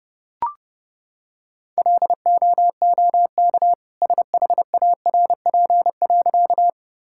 Morse code, heard as a keyed tone, about 30 wpm, 700 Hz. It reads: LOOK SHARP.
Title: Sentences from Top 1600 Words 30wpm